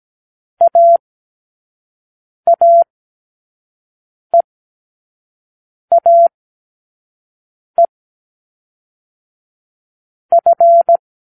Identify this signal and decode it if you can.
Morse code, heard as a keyed tone, about 17 wpm, 680 Hz. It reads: A A E A E F
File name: k7qo / Lesson9(A-F)